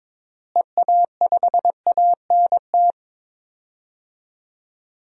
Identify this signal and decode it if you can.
Morse code, heard as a keyed tone, about 22 wpm, 700 Hz. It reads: EA5ANT
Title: DXCC-100 Call Signs - 6 Characters - Repeated 2x - 22wpm